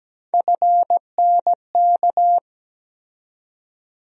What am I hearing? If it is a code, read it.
Morse code, heard as a keyed tone, about 17 wpm, 700 Hz. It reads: FNK
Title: USA Call Sign Postfixes - 3 Letters 17wpm